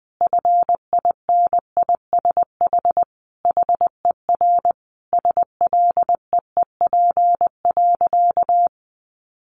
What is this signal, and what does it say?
Morse code, heard as a keyed tone, about 20 wpm, 700 Hz. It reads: FINISH HER SLEEP.
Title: Sentences from Top 1200 Words 20wpm